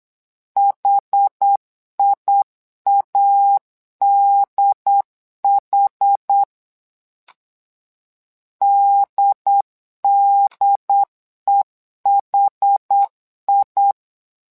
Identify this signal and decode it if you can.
Morse code, heard as a keyed tone, about 8 wpm, 800 Hz. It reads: HIADH DDEHI
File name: LEZIONI / 2 UHDN / 2 UHD 8wpm 2258